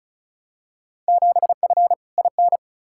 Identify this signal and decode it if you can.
Morse code, heard as a keyed tone, about 36 wpm, 690 Hz. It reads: 7F IN
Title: Field Day Exchanges Encore - 35wpm